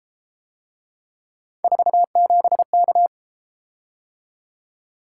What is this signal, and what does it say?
Morse code, heard as a keyed tone, about 30 wpm, 700 Hz. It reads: 47K